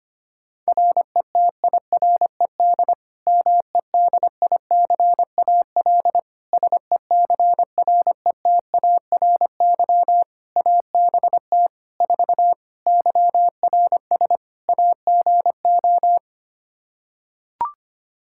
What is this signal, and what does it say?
Morse code, heard as a keyed tone, about 25 wpm, 700 Hz. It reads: RETIRED MEDICAL SECRETARY ABT 4 YRS AGO
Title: Practice QSO - Exchange 3 - More Info - 25wpm